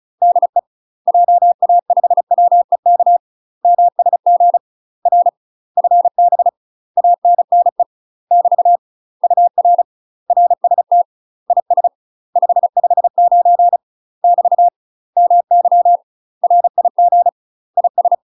Morse code, heard as a keyed tone, about 36 wpm, 700 Hz. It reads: DE JA5WEK MSG R FB ANNE <BT> UR RST IS 559 <BT> MY RIG IS
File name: UK MASTER QSOs TRAINING / 35WPM / UK MASTER QSOs 0036